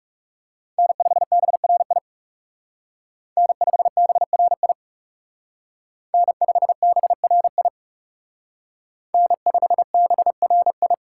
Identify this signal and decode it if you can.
Morse code, heard as a keyed tone, about 34 wpm, 700 Hz. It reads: N5BRI N5BRI N5BRI N5BRI
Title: Speed Racing - USA 1x3 Call Signs 30wpm